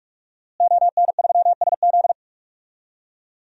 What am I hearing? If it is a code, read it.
Morse code, heard as a keyed tone, about 46 wpm, 700 Hz. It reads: ON3SZ
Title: DXCC-100 Call Signs - 5 Characters - Repeated 3x - 45wpm